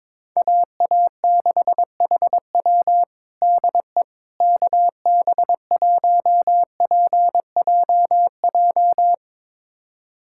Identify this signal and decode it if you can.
Morse code, heard as a keyed tone, about 22 wpm, 700 Hz. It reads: AA6HW DE KB1PJJ
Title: USA Callsign DE Callsign 22wpm